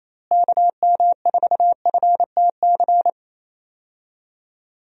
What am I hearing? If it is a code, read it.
Morse code, heard as a keyed tone, about 27 wpm, 700 Hz. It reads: KM4FTC
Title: USA Call Signs - 6 Characters - Repeated 3x - 28wpm